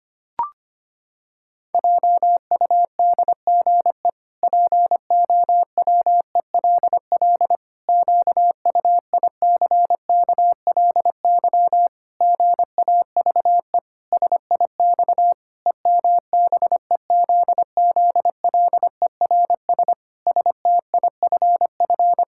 Morse code, heard as a keyed tone, about 25 wpm, 700 Hz. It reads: JUDGE POWELL QUICKLY GAVE SIX EMBEZZLERS STIFF
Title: Pangram Challenge v2 - 25wpm